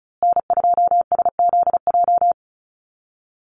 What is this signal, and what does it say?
Morse code, heard as a keyed tone, about 34 wpm, 700 Hz. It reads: N2SZJ
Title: USA Call Signs Repeated 2x 35wpm